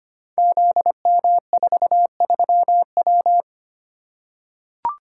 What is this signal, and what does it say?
Morse code, heard as a keyed tone, about 25 wpm, 700 Hz. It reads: ZM43W
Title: Code Groups - 5 Letter:Numbers - 25wpm